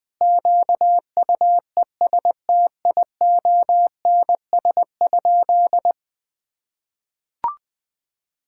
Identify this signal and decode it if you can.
Morse code, heard as a keyed tone, about 20 wpm, 700 Hz. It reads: QUESTIONS?
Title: Sentences with 4-Word Phrases - Spoken Beginning - 20wpm